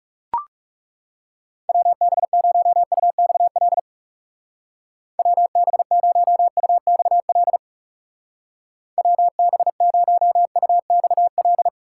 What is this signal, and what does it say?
Morse code, heard as a keyed tone, about 34 wpm, 700 Hz. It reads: WB0UXL WB0UXL WB0UXL
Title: Speed Racing - USA 2x3 Call Signs 30wpm